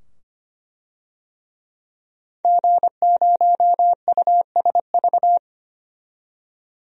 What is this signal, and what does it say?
Morse code, heard as a keyed tone, about 25 wpm, 700 Hz. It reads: G0USV
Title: DXCC-100 Call Signs - 5 Characters - 25wpm